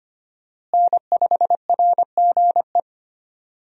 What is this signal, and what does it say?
Morse code, heard as a keyed tone, about 25 wpm, 700 Hz. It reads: N5RGE
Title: USA Call Signs - 5 Characters - Repeated 2x - 25wpm